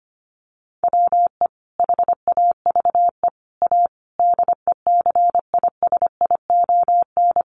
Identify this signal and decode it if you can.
Morse code, heard as a keyed tone, about 25 wpm, 700 Hz. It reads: WE HAVE A DECISION
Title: Sentences from Top 1000 Words 25wpm